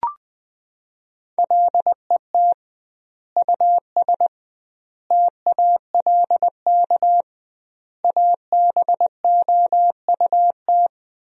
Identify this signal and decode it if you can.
Morse code, heard as a keyed tone, about 21 wpm, 700 Hz. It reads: LET US TALK ABOUT
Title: 2x Word-Spacing - Sentences from Top 400 Words - 20wpm